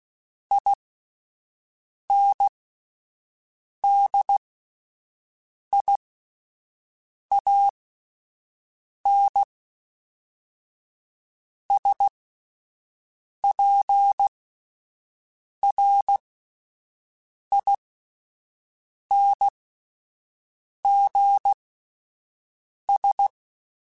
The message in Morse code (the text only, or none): INDIANSPRINGS